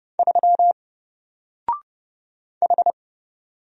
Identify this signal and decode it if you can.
Morse code, heard as a keyed tone, about 29 wpm, 700 Hz. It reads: 3 H
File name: Single-Letter-Number-30wpm